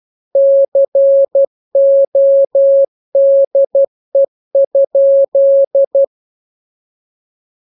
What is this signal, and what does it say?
Morse code, heard as a keyed tone, about 12 wpm, 550 Hz. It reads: CODE?
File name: jokes / joke 10 12 0000